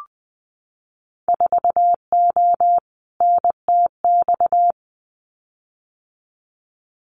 Morse code, heard as a keyed tone, about 20 wpm, 700 Hz. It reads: 4O NTX
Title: Winter Field Day Exchanges 20wpm